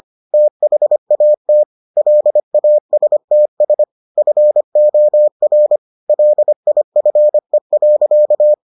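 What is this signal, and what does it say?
Morse code, heard as a keyed tone, about 25 wpm, 600 Hz. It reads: THAT LASTS FOR LIFE.
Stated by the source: Morse edition of the Foundations of Amateur Radio episodes.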